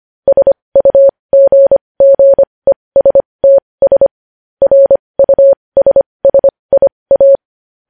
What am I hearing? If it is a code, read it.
Morse code, heard as a keyed tone, about 25 wpm, spaced as for 20 wpm, 550 Hz. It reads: SUGGESTS RUSSIA
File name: podcast / News headlines-1x-25x20